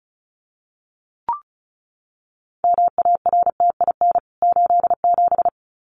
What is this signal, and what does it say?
Morse code, heard as a keyed tone, about 34 wpm, 700 Hz. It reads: MARTIN 87
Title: CWT Exchange 35wpm